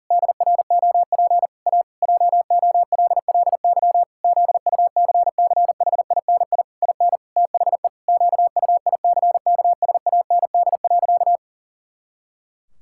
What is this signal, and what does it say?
Morse code, heard as a keyed tone, about 37 wpm, 700 Hz. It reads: DROP A JOLLY ZUCCHINI IN THE QUICKSAND.
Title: Pangram Challenge v2 - 40wpm